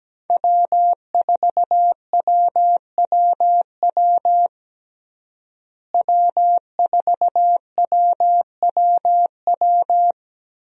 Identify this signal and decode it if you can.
Morse code, heard as a keyed tone, about 17 wpm, 700 Hz. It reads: W4WWW W4WWW
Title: USA Call Signs - 5 Characters - Repeated 2x - 17wpm